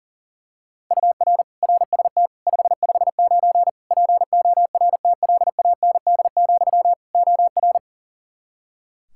Morse code, heard as a keyed tone, about 37 wpm, 700 Hz. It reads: UR RST 559 PORTLAND, OR
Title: K1USN SST QSO Practice 40wpm